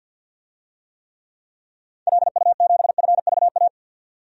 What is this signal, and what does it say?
Morse code, heard as a keyed tone, about 51 wpm, 700 Hz. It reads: LU7FVA
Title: DXCC-100 Call Signs - 6 Characters - 50wpm